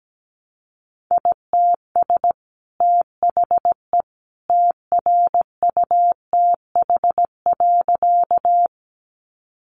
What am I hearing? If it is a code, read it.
Morse code, heard as a keyed tone, about 17 wpm, 700 Hz. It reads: ITS THE TRUTH.
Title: Sentences from Top 800 Words 17wpm